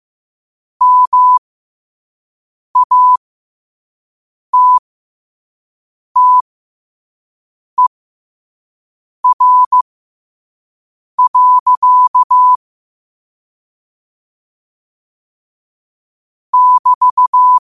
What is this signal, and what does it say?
Morse code, heard as a keyed tone, about 15 wpm, spaced as for 5 wpm, 1000 Hz. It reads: MATTER. <BT>
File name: qotdfiles / 5wpm